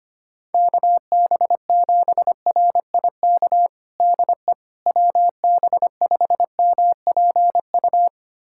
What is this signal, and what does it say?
Morse code, heard as a keyed tone, about 25 wpm, 700 Hz. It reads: KB7RIK DE WB5MPU
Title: USA Callsign DE Callsign - 6 Characters - 25wpm